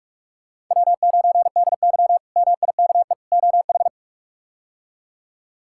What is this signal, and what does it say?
Morse code, heard as a keyed tone, about 43 wpm, 700 Hz. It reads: W9DY MIKE OH
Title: K1USN SST QSO Practice - 45wpm